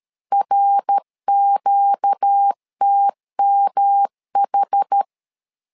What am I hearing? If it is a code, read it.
Morse code, heard as a keyed tone, about 13 wpm, 800 Hz. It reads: RQTMH
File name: LEZIONI / 4 Q 12wpm 1833